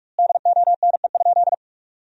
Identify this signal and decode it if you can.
Morse code, heard as a keyed tone, about 46 wpm, 700 Hz. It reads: DONE?
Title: 2x Word-Spacing - Sentences from Top 500 Words - 45wpm